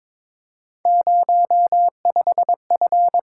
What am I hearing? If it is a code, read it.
Morse code, heard as a keyed tone, about 22 wpm, 700 Hz. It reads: 05F
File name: practice / 22wpm / Three Letter-Number Combinations - 22wpm